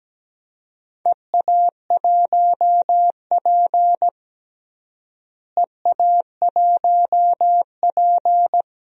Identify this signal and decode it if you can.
Morse code, heard as a keyed tone, about 17 wpm, 700 Hz. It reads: EA1P EA1P
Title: DXCC-100 Call Signs - 4 Characters - Repeated 3x - 17wpm